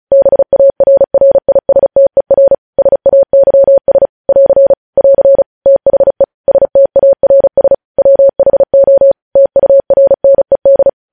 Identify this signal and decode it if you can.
Morse code, heard as a keyed tone, about 36 wpm, 550 Hz. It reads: BARRISTER SAYS <AR> <AR> THE STARS WHO TURNED
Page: news headlines in morse code at 35 wpm